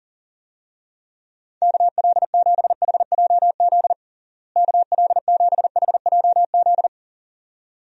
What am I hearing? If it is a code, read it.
Morse code, heard as a keyed tone, about 40 wpm, 700 Hz. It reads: KL7HJZ KL7HJZ
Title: DXCC-100 Call Signs - 6 Characters - Repeated 2x - 40wpm